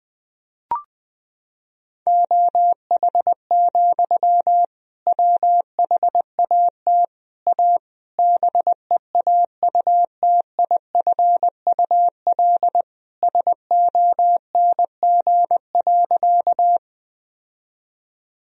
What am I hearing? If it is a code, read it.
Morse code, heard as a keyed tone, about 20 wpm, 700 Hz. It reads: OH, WHAT A BEAUTIFUL SONG.
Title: Sentences from Top 1000 Words - Review Part 2 - 20wpm